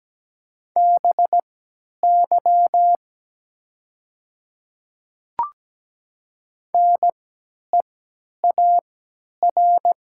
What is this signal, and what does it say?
Morse code, heard as a keyed tone, about 17 wpm, spaced as for 10 wpm, 700 Hz. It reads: BY NEAR
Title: 3x Character Spacing - Top 300 Words - 17wpm